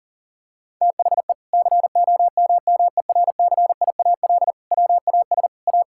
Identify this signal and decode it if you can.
Morse code, heard as a keyed tone, about 40 wpm, 700 Hz. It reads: THE COMMERCIAL WAS A
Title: Sentences from Top 1000 Words - Review Part 5 - 40wpm